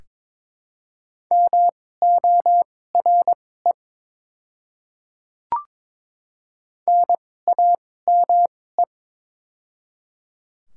Morse code, heard as a keyed tone, about 22 wpm, spaced as for 16 wpm, 700 Hz. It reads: MORE NAME
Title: 2x Character Spacing - Top 500 Words - 22wpm